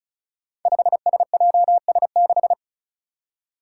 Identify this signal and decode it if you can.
Morse code, heard as a keyed tone, about 36 wpm, 700 Hz. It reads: 5SJS6
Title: Five Challenging Letter-Number Combinations 35wpm